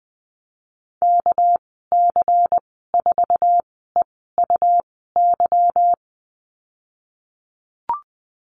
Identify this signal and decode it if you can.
Morse code, heard as a keyed tone, about 20 wpm, spaced as for 15 wpm, 700 Hz. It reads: KC4EUY
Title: 200 Lesson 32g - Letter k - Call Signs 20wpm